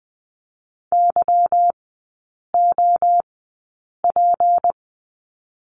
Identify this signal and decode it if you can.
Morse code, heard as a keyed tone, about 20 wpm, spaced as for 8 wpm, 700 Hz. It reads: YOP
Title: 152 Lesson 25e - Letter p - Three Characters 20wpm